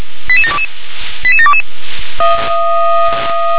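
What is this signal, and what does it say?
Five-tone selective call (ZVEI), heard as quick runs of five tones, tones of about 70 ms. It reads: 88722 88722
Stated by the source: ZVEI five-tone selective call received by SDR.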